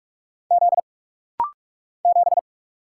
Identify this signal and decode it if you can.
Morse code, heard as a keyed tone, about 44 wpm, 700 Hz. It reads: Z 7
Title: Single Letter-Number - Rapid-Fire 45wpm